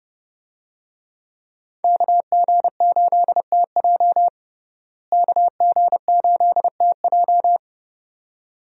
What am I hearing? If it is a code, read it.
Morse code, heard as a keyed tone, about 32 wpm, 700 Hz. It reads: KG8TJ KG8TJ
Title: USA Call Signs - 5 Characters - Repeated 2x - 30wpm